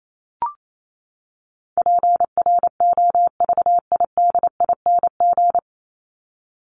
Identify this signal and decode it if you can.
Morse code, heard as a keyed tone, about 28 wpm, 700 Hz. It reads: PROVIDING
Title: ING Suffix - Top 100 Words with Base Words 28wpm